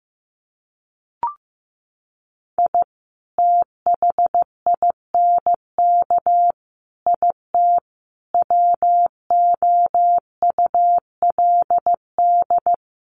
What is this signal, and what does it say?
Morse code, heard as a keyed tone, about 15 wpm, 700 Hz. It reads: I THINK IT WOULD
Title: Sets of 5 Words 15wpm